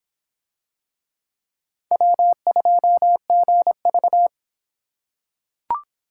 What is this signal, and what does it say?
Morse code, heard as a keyed tone, about 23 wpm, 700 Hz. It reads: W2GV